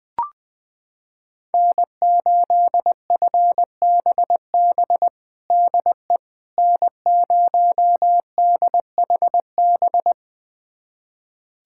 Morse code, heard as a keyed tone, about 21 wpm, 700 Hz. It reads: N8FBB DE N0DHB
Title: USA Callsign DE Callsign - 5 Characters - 20wpm